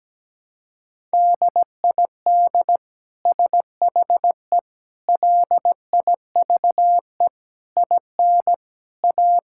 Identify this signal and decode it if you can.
Morse code, heard as a keyed tone, about 17 wpm, 700 Hz. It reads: DID SHE LIVE IN A
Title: Sentences from Top 1000 Words - Review Part 5 - 17wpm